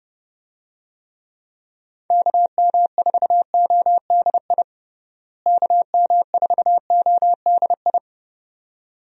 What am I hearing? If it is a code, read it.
Morse code, heard as a keyed tone, about 31 wpm, 700 Hz. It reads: KM4ODI KM4ODI
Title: USA Call Signs - 6 Characters - Repeated 2x - 30wpm